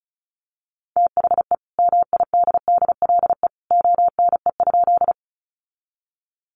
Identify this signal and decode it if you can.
Morse code, heard as a keyed tone, about 34 wpm, 700 Hz. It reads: THE MIDDLE ONE?
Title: Sentences from Top 900 Words 35wpm